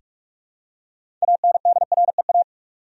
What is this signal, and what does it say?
Morse code, heard as a keyed tone, about 45 wpm, 700 Hz. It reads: ANDREA
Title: USA Common Names - 45wpm